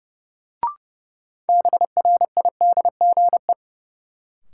Morse code, heard as a keyed tone, about 30 wpm, 700 Hz. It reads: BRIDGE
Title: Sentences from Top 1200 Words - New Word Drill Alt - 30wpm